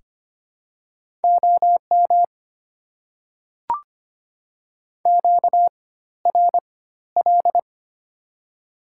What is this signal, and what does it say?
Morse code, heard as a keyed tone, about 25 wpm, spaced as for 14 wpm, 700 Hz. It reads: OM QRL
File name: Top 90 QSO Elements - 4x Character Spacing - 25wpm